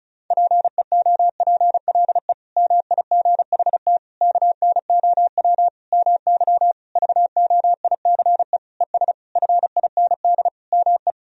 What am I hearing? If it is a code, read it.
Morse code, heard as a keyed tone, about 36 wpm, 700 Hz. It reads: PEOPLE MIGHT KNOW MY VOICE ES FIND ME
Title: Sentences from Top 500 Words and Abbreviations 35wpm